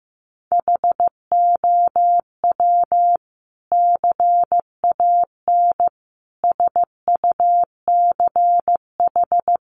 Morse code, heard as a keyed tone, about 15 wpm, 700 Hz. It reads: HOW CAN SUCH
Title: Sentences from Top 1100 Words 15wpm